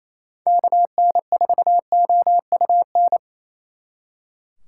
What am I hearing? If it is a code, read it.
Morse code, heard as a keyed tone, about 37 wpm, 700 Hz. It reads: KN4OUN